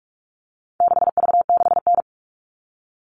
Morse code, heard as a keyed tone, about 44 wpm, 700 Hz. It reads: 6V6N
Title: All Cut Numbers - 4-Digit 45wpm